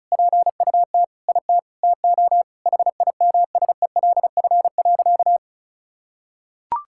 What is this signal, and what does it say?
Morse code, heard as a keyed tone, about 34 wpm, 700 Hz. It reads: PUT IT TO HIMSELF.
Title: Sentences from Top 1600 Words 35wpm